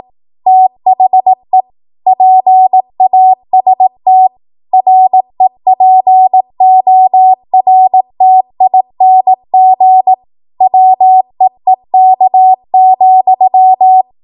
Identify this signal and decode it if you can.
Morse code, heard as a keyed tone, about 18 wpm, 750 Hz. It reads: THE PAST REPORTING WEEK,
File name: Archive / 18 WPM / 18WPM